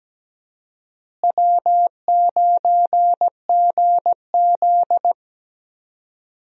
Morse code, heard as a keyed tone, about 17 wpm, 700 Hz. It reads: W9GZ
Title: USA Call Signs - 4 Characters - Repeated 3x - 17wpm